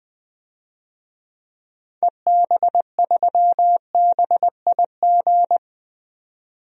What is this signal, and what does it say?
Morse code, heard as a keyed tone, about 21 wpm, 700 Hz. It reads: EB3BIG
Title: DXCC-100 Call Signs - 6 Characters - 20wpm